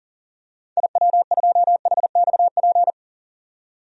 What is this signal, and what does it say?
Morse code, heard as a keyed tone, about 40 wpm, 700 Hz. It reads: IW2HXP
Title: DXCC-100 Call Signs - 6 Characters - 40wpm